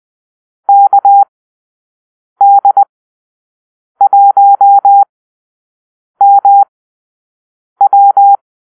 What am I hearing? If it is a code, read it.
Morse code, heard as a keyed tone, about 20 wpm, spaced as for 6 wpm, 800 Hz. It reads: KD1MW
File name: arrl / Call-Sign Drill-6